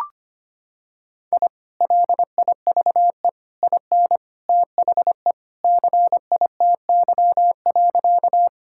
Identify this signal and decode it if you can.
Morse code, heard as a keyed tone, about 25 wpm, 700 Hz. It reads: I LIVE IN THE CITY.
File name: practice / 25wpm / Sentences with 2-Word Phrases - 25wpm